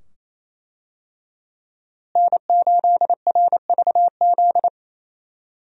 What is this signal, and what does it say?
Morse code, heard as a keyed tone, about 27 wpm, 700 Hz. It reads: N8RVZ
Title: USA Call Signs - 5 Characters - Repeated 2x - 28wpm